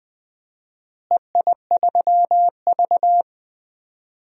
Morse code, heard as a keyed tone, about 20 wpm, 700 Hz. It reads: EI3V
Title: DXCC-100 Call Signs - 4 Characters - Repeated 3x - 20wpm